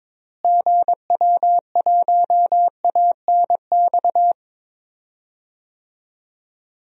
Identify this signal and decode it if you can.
Morse code, heard as a keyed tone, about 22 wpm, 700 Hz. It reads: GW1ANX
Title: DXCC-100 Call Signs - 6 Characters - Repeated 3x - 22wpm